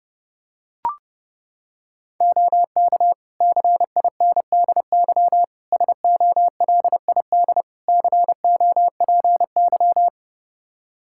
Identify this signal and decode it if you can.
Morse code, heard as a keyed tone, about 32 wpm, 700 Hz. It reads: OK CINDY SOLID COPY
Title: K1USN SST QSO Practice 30wpm